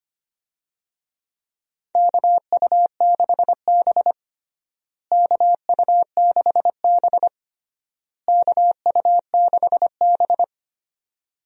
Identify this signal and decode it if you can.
Morse code, heard as a keyed tone, about 25 wpm, 700 Hz. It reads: KU6B KU6B KU6B
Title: USA Call Signs - 4 Characters - Repeated 3x - 25wpm